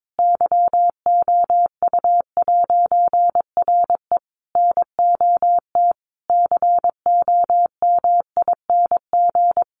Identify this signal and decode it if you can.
Morse code, heard as a keyed tone, about 22 wpm, 700 Hz. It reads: YOU'RE NOT COMING
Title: A Sound Of Thunder 22wpm